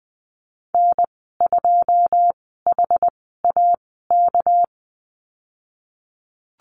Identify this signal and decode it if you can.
Morse code, heard as a keyed tone, about 20 wpm, spaced as for 15 wpm, 700 Hz. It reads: N2HAK
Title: 200 Lesson 32g - Letter k - Call Signs 20wpm